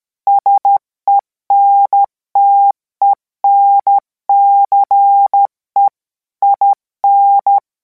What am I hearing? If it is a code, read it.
Morse code, heard as a keyed tone, about 13 wpm, 800 Hz. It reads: SENTENCE IN